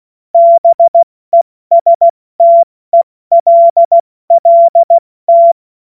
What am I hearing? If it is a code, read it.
Morse code, heard as a keyed tone, about 16 wpm, 680 Hz. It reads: BESTELLT
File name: Morsekurs / Witze / Witz 11. Mai 2021, 70 F80